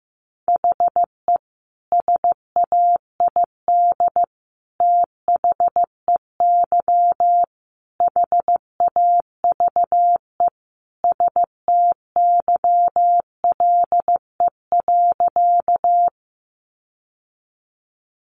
Morse code, heard as a keyed tone, about 15 wpm, 700 Hz. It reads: HE SAID THEY HAVE STYLE.
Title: Sentences from Top 800 Words 15wpm